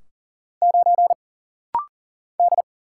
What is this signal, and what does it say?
Morse code, heard as a keyed tone, about 40 wpm, 700 Hz. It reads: 9 D
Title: Single Letter-Number Alt - Rapid-Fire 40wpm